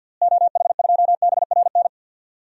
Morse code, heard as a keyed tone, about 49 wpm, 700 Hz. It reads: OH2BRN
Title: DXCC-100 Call Signs - 6 Characters - Repeated 3x - 50wpm